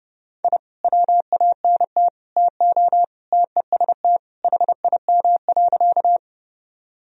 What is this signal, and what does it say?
Morse code, heard as a keyed tone, about 32 wpm, 700 Hz. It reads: I WANT TO TEST HIM.
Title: Sentences from Top 1000 Words - Review Part 3 - 30wpm